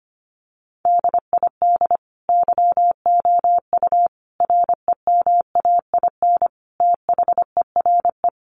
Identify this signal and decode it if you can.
Morse code, heard as a keyed tone, about 25 wpm, 700 Hz. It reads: DID YOU REMAIN THERE
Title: Sentences from Top 1200 Words 25wpm